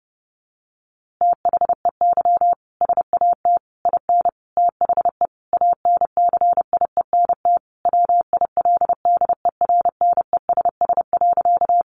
Morse code, heard as a keyed tone, about 30 wpm, 700 Hz. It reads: THEY SAT IN THE ANCIENT WILDERNESS.
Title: A Sound Of Thunder 30wpm